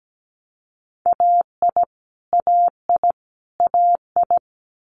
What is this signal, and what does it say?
Morse code, heard as a keyed tone, about 17 wpm, 690 Hz. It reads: AI AI AI
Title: Multi-letter Phonemes Repeated 3x 17wpm